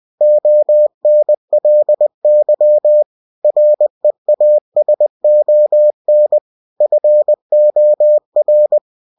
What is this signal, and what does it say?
Morse code, heard as a keyed tone, about 20 wpm, 600 Hz. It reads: ONLY REASON FOR
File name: cw-books / winnie-the-pooh / winnie-20wpm-page-0002